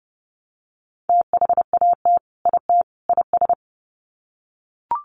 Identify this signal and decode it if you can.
Morse code, heard as a keyed tone, about 30 wpm, 700 Hz. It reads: THAT IT IS
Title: Sets of 3 Words 30wpm